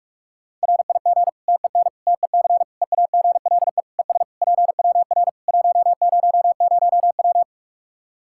Morse code, heard as a keyed tone, about 46 wpm, 700 Hz. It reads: RIG TEN TEC EAGLE ES PWR 100W
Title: Practice QSO - Exchange 2 - Info - 45wpm